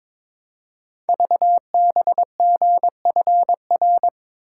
Morse code, heard as a keyed tone, about 23 wpm, 700 Hz. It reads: VBGFR
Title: Code Groups - 5 Letters - 22wpm